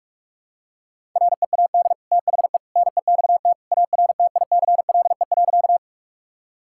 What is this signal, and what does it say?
Morse code, heard as a keyed tone, about 41 wpm, 700 Hz. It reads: READ THE NEXT ARTICLE.
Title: Sentences from Top 1000 Words - Review Part 5 - 45wpm